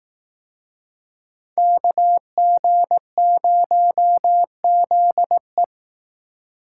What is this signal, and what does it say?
Morse code, heard as a keyed tone, about 18 wpm, 700 Hz. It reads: KG0ZE
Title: Speed Racing - USA 2x2 Call Signs 15wpm